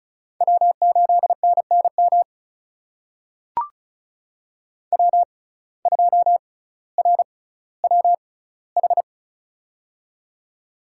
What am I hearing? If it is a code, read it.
Morse code, heard as a keyed tone, about 36 wpm, spaced as for 18 wpm, 700 Hz. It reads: W8NNM W2RWH
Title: USA Call Signs - 6x Character Spacing - 35wpm